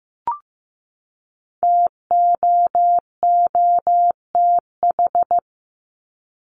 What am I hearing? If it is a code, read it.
Morse code, heard as a keyed tone, about 15 wpm, 700 Hz. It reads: TOOTH
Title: Top 1751-2000 Words 15wpm